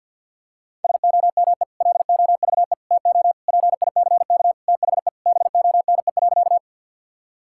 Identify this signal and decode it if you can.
Morse code, heard as a keyed tone, about 52 wpm, 700 Hz. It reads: SOME LOVE TO PICK THE BONE.
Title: Sentences from Top 1600 Words 50wpm